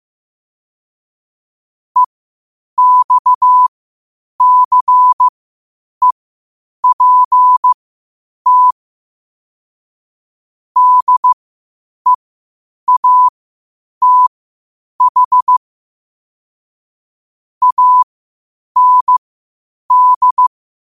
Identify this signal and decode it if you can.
Morse code, heard as a keyed tone, about 15 wpm, spaced as for 8 wpm, 1000 Hz. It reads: EXCEPT DEATH AND
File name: qotdfiles / 8wpm